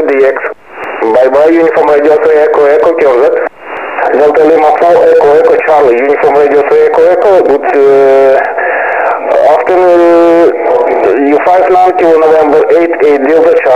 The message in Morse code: none